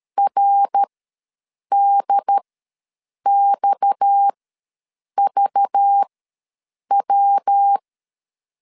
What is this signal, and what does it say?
Morse code, heard as a keyed tone, about 13 wpm, spaced as for 7 wpm, 800 Hz. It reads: RDXVW
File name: LEZIONI / 7 GF9Z / 7 G ws 12wpm 3140